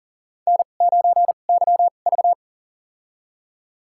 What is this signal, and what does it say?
Morse code, heard as a keyed tone, about 40 wpm, spaced as for 29 wpm, 690 Hz. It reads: N9YV
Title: USA Call Signs - 4 Characters - 2x Character Spacing - 40wpm